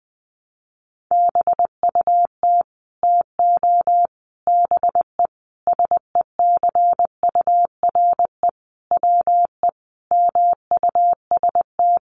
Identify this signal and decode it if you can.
Morse code, heard as a keyed tone, about 19 wpm, 700 Hz. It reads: BUT TO BE SECURE WE MUST
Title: Sentences from Top 1200 Words 20wpm